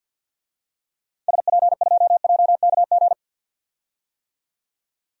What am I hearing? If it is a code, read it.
Morse code, heard as a keyed tone, about 52 wpm, 700 Hz. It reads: SP2JKG
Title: DXCC-100 Call Signs - 6 Characters - 50wpm